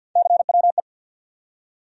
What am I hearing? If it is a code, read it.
Morse code, heard as a keyed tone, about 47 wpm, 700 Hz. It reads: CWE
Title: USA Call Sign Postfixes - 3 Letters 50wpm